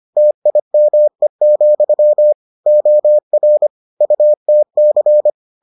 Morse code, heard as a keyed tone, about 25 wpm, 600 Hz. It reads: TIME, OR UTC